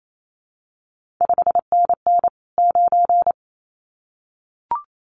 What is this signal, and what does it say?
Morse code, heard as a keyed tone, about 28 wpm, 700 Hz. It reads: 5NN 9
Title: CQ WW DX Contest Exchange Encore - 28wpm